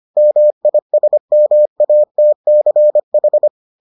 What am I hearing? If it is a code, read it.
Morse code, heard as a keyed tone, about 25 wpm, 600 Hz. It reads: MISMATCH